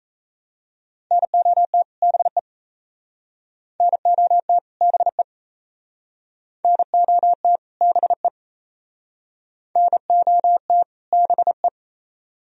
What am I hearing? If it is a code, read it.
Morse code, heard as a keyed tone, about 35 wpm, 700 Hz. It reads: NOT BE NOT BE NOT BE NOT BE